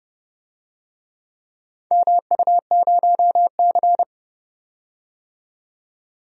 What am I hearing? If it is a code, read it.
Morse code, heard as a keyed tone, about 32 wpm, 700 Hz. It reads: MU0C
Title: DXCC-100 Call Signs - 4 Characters - 30wpm